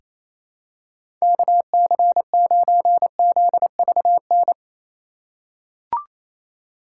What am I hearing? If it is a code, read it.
Morse code, heard as a keyed tone, about 28 wpm, 700 Hz. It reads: KC9ZVN
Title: USA Call Signs - 6 Characters - 2x Character Spacing - 28wpm